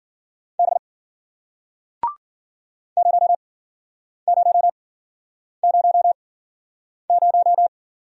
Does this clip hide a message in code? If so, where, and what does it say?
Yes: Morse code from 0.6 to 7.7 s, heard as a keyed tone, about 51 wpm, spaced as for 9 wpm, 700 Hz. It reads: B 0000